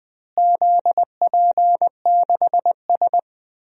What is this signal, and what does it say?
Morse code, heard as a keyed tone, about 20 wpm, 700 Hz. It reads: ZP6S
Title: DXCC-100 Call Signs - 4 Characters - Repeated 3x - 20wpm